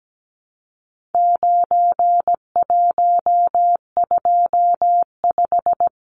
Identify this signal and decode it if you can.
Morse code, heard as a keyed tone, about 17 wpm, 700 Hz. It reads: 9125
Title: Numbers - 4-Digit 17wpm